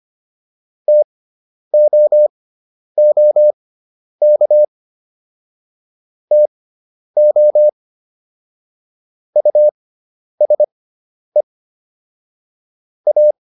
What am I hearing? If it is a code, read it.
Morse code, heard as a keyed tone, about 25 wpm, spaced as for 10 wpm, 600 Hz. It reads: TOOK TO USE A